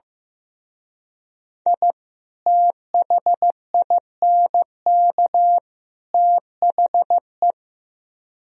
Morse code, heard as a keyed tone, about 17 wpm, 700 Hz. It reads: I THINK THE
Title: Speed-Racing - Sets of 3 Words 15wpm